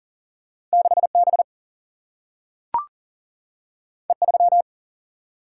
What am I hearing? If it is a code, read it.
Morse code, heard as a keyed tone, about 39 wpm, 700 Hz. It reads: 6B E3
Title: Single-Letter Single-Number 40wpm